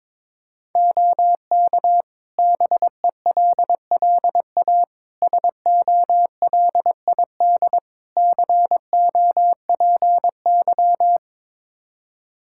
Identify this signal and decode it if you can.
Morse code, heard as a keyed tone, about 22 wpm, 700 Hz. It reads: OK BELLA SOLID COPY
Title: K1USN SST QSO Practice 22wpm